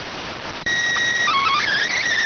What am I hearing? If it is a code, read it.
SSTV (slow-scan television), PD 290.